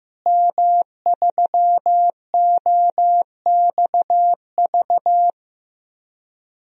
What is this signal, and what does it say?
Morse code, heard as a keyed tone, about 15 wpm, 700 Hz. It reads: M3OXV
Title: DXCC-100 Call Signs - 5 Characters - Repeated 3x - 15wpm